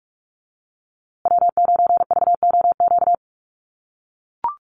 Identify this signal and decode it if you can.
Morse code, heard as a keyed tone, about 44 wpm, 700 Hz. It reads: W9VOQ
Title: USA Call Signs Repeated 3x 45wpm